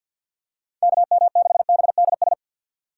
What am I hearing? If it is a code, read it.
Morse code, heard as a keyed tone, about 49 wpm, 700 Hz. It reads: KM6BDS
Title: Speed Racing - USA 2x3 Call Signs 50wpm